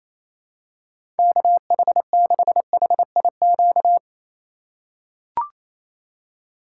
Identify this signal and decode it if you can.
Morse code, heard as a keyed tone, about 28 wpm, 700 Hz. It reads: KH6HIQ
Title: USA Call Signs - 3x Character Spacing - 28wpm